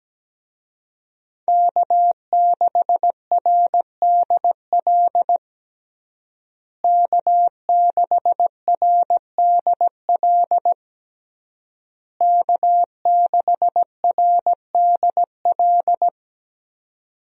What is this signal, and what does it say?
Morse code, heard as a keyed tone, about 17 wpm, 700 Hz. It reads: K6RDL K6RDL K6RDL
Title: USA Call Signs - 5 Characters - Repeated 3x - 17wpm